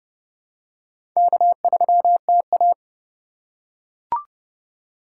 Morse code, heard as a keyed tone, about 30 wpm, 700 Hz. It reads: K3TA
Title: USA Call Signs - 2x Character Spacing - 30wpm